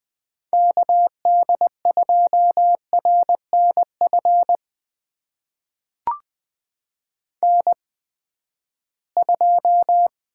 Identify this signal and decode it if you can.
Morse code, heard as a keyed tone, about 20 wpm, spaced as for 14 wpm, 700 Hz. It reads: KD2RNF N2
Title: USA Call Signs - 8x Character Spacing - 20wpm